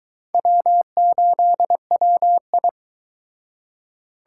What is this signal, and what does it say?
Morse code, heard as a keyed tone, about 24 wpm, 700 Hz. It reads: W8WI